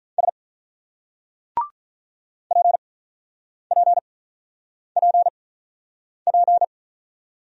Morse code, heard as a keyed tone, about 47 wpm, spaced as for 9 wpm, 700 Hz. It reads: S PPPP